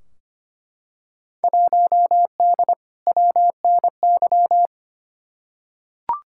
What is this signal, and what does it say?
Morse code, heard as a keyed tone, about 25 wpm, 700 Hz. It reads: 1D WNY
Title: Field Day Exchanges - 25wpm